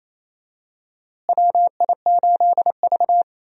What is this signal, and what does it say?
Morse code, heard as a keyed tone, about 28 wpm, 700 Hz. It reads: WI8V
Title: USA Call Signs - 4 Characters - Repeated 3x - 28wpm